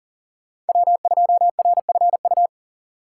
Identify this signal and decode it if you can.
Morse code, heard as a keyed tone, about 40 wpm, 700 Hz. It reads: W2RFU
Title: USA Call Signs - 5 Characters - 4x Character Spacing - 40wpm